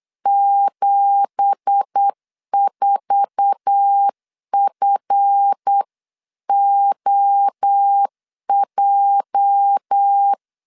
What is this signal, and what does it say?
Morse code, heard as a keyed tone, about 8 wpm, 800 Hz. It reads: 74FOJ
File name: LEZIONI / 9 2867 8wpm 3440